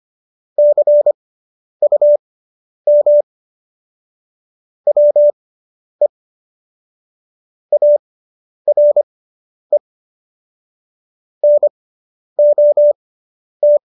Morse code, heard as a keyed tone, about 25 wpm, spaced as for 10 wpm, 590 Hz. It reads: CUM WE ARE NOT